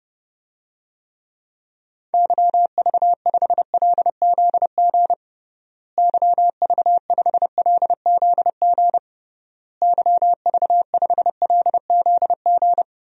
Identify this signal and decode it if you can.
Morse code, heard as a keyed tone, about 30 wpm, 700 Hz. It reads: YV5LZG YV5LZG YV5LZG
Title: DXCC-100 Call Signs - 6 Characters - Repeated 3x - 30wpm